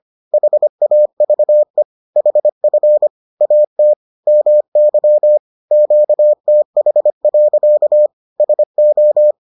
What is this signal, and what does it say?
Morse code, heard as a keyed tone, about 25 wpm, 600 Hz. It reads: HAVE HF AT MY QTH. SO